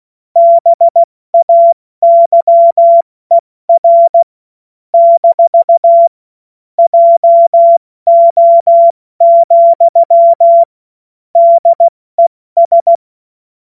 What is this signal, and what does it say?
Morse code, heard as a keyed tone, about 16 wpm, spaced as for 14 wpm, 680 Hz. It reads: BAYER - JO, DES
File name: Witz-12-August-2025-70-F80